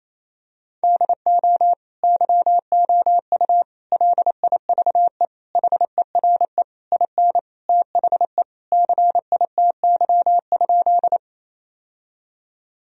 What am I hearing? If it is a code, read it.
Morse code, heard as a keyed tone, about 28 wpm, 700 Hz. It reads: DO YOU LIVE HERE IN THE CITY?
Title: Sentences from Top 1000 Words - Review Part 1 - 28wpm